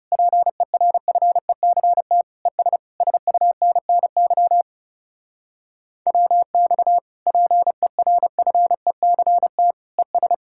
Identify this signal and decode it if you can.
Morse code, heard as a keyed tone, about 36 wpm, 700 Hz. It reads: PERFECT ES SUNNY WX PERFECT ES
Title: Speed-Racing Common QSO Phrases - 30wpm